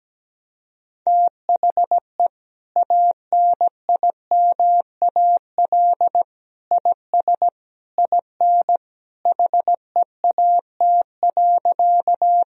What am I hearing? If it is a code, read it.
Morse code, heard as a keyed tone, about 17 wpm, 700 Hz. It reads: THE ANIMAL IS IN HEAT.
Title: Sentences from Top 1000 Words - Review Part 3 - 17wpm